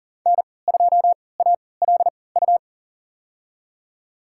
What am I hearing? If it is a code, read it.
Morse code, heard as a keyed tone, about 40 wpm, spaced as for 23 wpm, 700 Hz. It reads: N2ALU
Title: USA Call Signs - 5 Characters - 3x Character Spacing - 40wpm